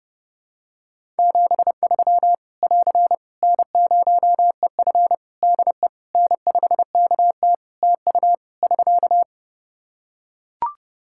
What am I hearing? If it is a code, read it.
Morse code, heard as a keyed tone, about 30 wpm, 700 Hz. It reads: 73 <AR> N0EF DE N5KT TU <SK>
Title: Practice QSO - Exchange 4 - Ending - 30wpm